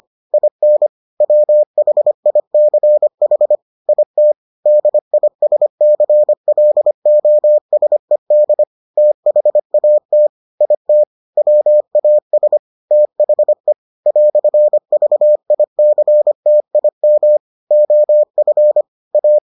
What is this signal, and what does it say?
Morse code, heard as a keyed tone, about 25 wpm, 600 Hz. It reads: IN WHICH IT DISCLOSED THAT IT WAS THE "VICTIM OF A